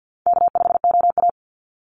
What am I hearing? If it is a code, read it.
Morse code, heard as a keyed tone, about 48 wpm, 700 Hz. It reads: K5OA
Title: Call Signs - 50wpm